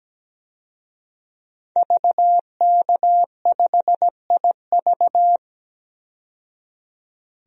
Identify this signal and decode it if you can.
Morse code, heard as a keyed tone, about 17 wpm, 700 Hz. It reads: VK5IV
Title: DXCC-100 Call Signs - 5 Characters - 17wpm